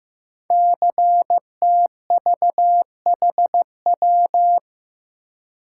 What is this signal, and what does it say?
Morse code, heard as a keyed tone, about 15 wpm, 700 Hz. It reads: CTVHW
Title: Code Groups - 5 Letters - 15wpm